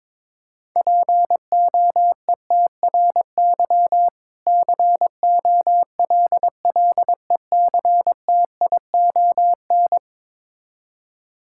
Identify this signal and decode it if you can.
Morse code, heard as a keyed tone, about 22 wpm, 700 Hz. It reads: POETRY COLLECTION
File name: Difficult Words with Hints - 22wpm